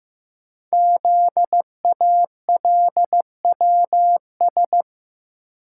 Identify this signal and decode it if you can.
Morse code, heard as a keyed tone, about 15 wpm, 700 Hz. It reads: ZALWS
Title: Code Groups - 5 Letters - 15wpm